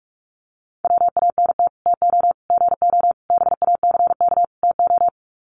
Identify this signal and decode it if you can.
Morse code, heard as a keyed tone, about 44 wpm, 700 Hz. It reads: WANT TO GO BACK TO
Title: Sets of 5 Words 45wpm